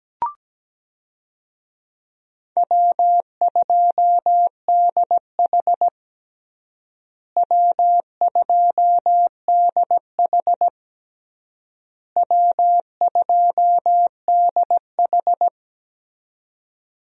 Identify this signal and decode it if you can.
Morse code, heard as a keyed tone, about 17 wpm, 700 Hz. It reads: W2DH W2DH W2DH
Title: USA Call Signs - 4 Characters - Repeated 3x - 17wpm